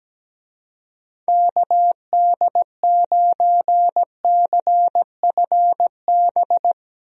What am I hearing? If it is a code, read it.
Morse code, heard as a keyed tone, about 17 wpm, 700 Hz. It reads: KD9CFB
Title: USA Call Signs - 6 Characters - Repeated 2x - 17wpm